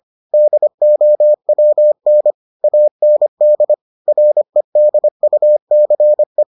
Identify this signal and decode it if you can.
Morse code, heard as a keyed tone, about 25 wpm, 600 Hz. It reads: DOWN AND REDUCE